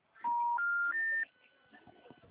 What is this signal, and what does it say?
Special information tone.
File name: out- -2024- - - .57570